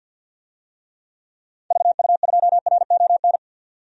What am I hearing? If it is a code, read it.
Morse code, heard as a keyed tone, about 52 wpm, 700 Hz. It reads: VU2RON
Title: DXCC-100 Call Signs - 6 Characters - 50wpm